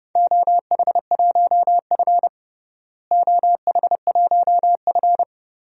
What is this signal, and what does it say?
Morse code, heard as a keyed tone, about 30 wpm, 700 Hz. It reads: OH1F OH1F
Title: DXCC-100 Call Signs - 4 Characters - Repeated 3x - 30wpm